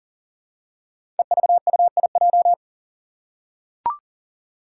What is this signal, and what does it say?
Morse code, heard as a keyed tone, about 40 wpm, 700 Hz. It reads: EVUIJ
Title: Five Challenging Letter-Number Combinations 40wpm